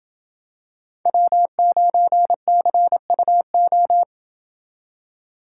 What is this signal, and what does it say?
Morse code, heard as a keyed tone, about 27 wpm, 700 Hz. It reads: W9CUO